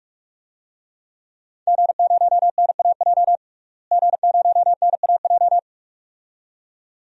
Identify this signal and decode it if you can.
Morse code, heard as a keyed tone, about 46 wpm, 700 Hz. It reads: G0NAJ G0NAJ
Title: DXCC-100 Call Signs - 5 Characters - Repeated 2x - 45wpm